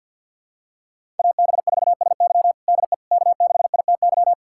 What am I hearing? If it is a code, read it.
Morse code, heard as a keyed tone, about 51 wpm, 700 Hz. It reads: AB4SY DE K6ITY
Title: USA Callsign DE Callsign - 5 Characters - 50wpm